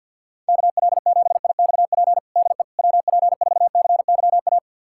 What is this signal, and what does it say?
Morse code, heard as a keyed tone, about 51 wpm, 700 Hz. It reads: KL7IXP DE WP4CYA
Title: USA Callsign DE Callsign - 6 Characters - 50wpm